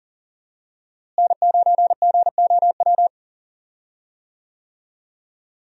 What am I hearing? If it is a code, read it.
Morse code, heard as a keyed tone, about 40 wpm, 700 Hz. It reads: N9GOW
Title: USA Call Signs - 5 Characters - Repeated 2x - 40wpm